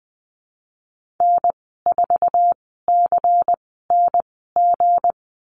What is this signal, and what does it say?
Morse code, heard as a keyed tone, about 19 wpm, spaced as for 14 wpm, 700 Hz. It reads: N4CNG
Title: 193 Lesson 31g - Letter v - Call Signs 20wpm